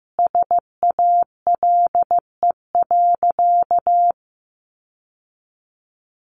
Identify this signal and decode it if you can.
Morse code, heard as a keyed tone, about 15 wpm, 700 Hz. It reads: SALE.
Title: Sentences from Top 1400 Words 15wpm